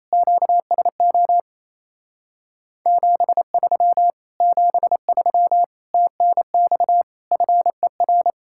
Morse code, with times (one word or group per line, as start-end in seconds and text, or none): 0.1-1.4 QSO
2.9-4.1 73
4.4-5.6 73
5.9-7.0 TNX
7.3-8.3 FER